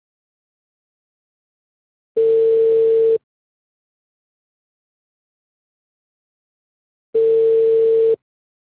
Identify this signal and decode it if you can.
Ringback tone (China, 450 Hz).